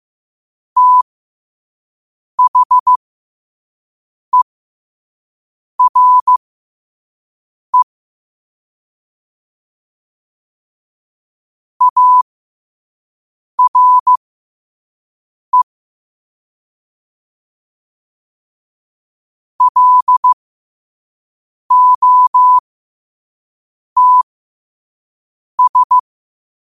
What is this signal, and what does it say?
Morse code, heard as a keyed tone, about 15 wpm, spaced as for 5 wpm, 1000 Hz. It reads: THERE ARE LOTS
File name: qotdfiles / 5wpm